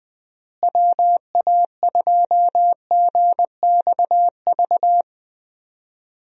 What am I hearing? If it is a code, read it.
Morse code, heard as a keyed tone, about 21 wpm, 700 Hz. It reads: WA2GXV
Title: USA Call Signs - 6 Characters - Repeated 2x - 20wpm